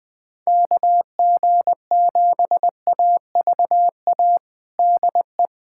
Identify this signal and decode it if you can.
Morse code, heard as a keyed tone, about 20 wpm, 700 Hz. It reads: KG7AVA DE
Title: USA Callsign DE Callsign - 6 Characters - 20wpm